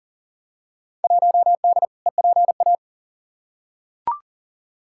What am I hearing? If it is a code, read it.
Morse code, heard as a keyed tone, about 40 wpm, 700 Hz. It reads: 1D EPA
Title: Field Day Exchanges - 40wpm